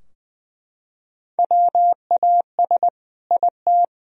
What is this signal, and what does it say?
Morse code, heard as a keyed tone, about 21 wpm, 700 Hz. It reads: WAS IT
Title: Sentences from Top 1000 Words - Review Part 4 - 20wpm